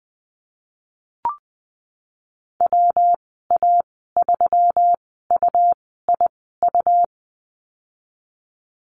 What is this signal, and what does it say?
Morse code, heard as a keyed tone, about 20 wpm, spaced as for 15 wpm, 700 Hz. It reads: WA3UIU
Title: 167 Lesson 27g - Number 7 - Call Signs 20wpm